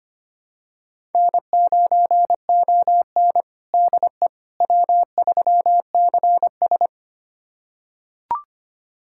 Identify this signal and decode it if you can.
Morse code, heard as a keyed tone, about 25 wpm, 700 Hz. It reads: N9ON DE W3CS
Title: USA Callsign DE Callsign - 4 Characters - 25wpm